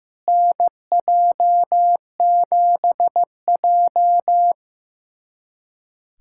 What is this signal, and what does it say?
Morse code, heard as a keyed tone, about 15 wpm, 700 Hz. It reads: NJ7J
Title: USA Callsign DE Callsign - 4 Characters - 15wpm